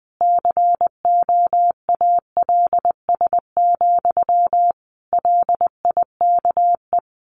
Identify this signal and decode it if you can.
Morse code, heard as a keyed tone, about 20 wpm, 700 Hz. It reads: COALS, LIKE
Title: A Sound Of Thunder 20wpm